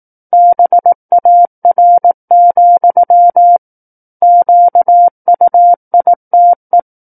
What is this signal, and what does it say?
Morse code, heard as a keyed tone, about 18 wpm, 700 Hz. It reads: BAR, QUITE
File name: 18 WPM